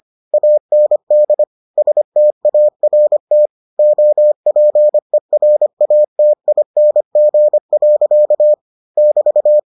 Morse code, heard as a keyed tone, about 25 wpm, 600 Hz. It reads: AND START OPERATING. <BT>